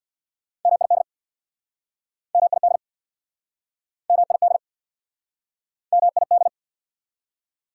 Morse code, heard as a keyed tone, about 63 wpm, 690 Hz. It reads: MID MID MID MID